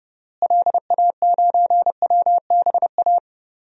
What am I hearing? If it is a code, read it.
Morse code, heard as a keyed tone, about 30 wpm, 700 Hz. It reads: LA9WBA
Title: DXCC-100 Call Signs - 6 Characters - Repeated 3x - 30wpm